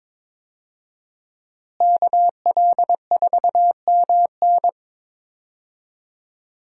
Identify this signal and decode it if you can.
Morse code, heard as a keyed tone, about 22 wpm, 700 Hz. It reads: KL4MN
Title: USA Call Signs - 5 Characters - Repeated 2x - 22wpm